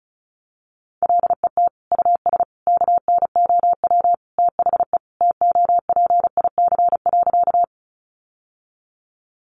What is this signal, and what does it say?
Morse code, heard as a keyed tone, about 34 wpm, 700 Hz. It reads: LET US KNOW THE TOPIC.
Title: Sentences from Top 1200 Words 35wpm